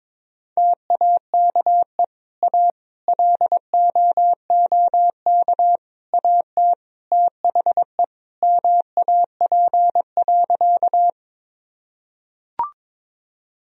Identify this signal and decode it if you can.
Morse code, heard as a keyed tone, about 22 wpm, 700 Hz. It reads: TAKE A LOOK AT THE MAP.
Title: Sentences from Top 1000 Words - Review Part 2 - 22wpm